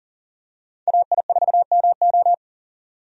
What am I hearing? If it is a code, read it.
Morse code, heard as a keyed tone, about 40 wpm, 700 Hz. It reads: AI4MO
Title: USA Call Signs - 5 Characters - 40wpm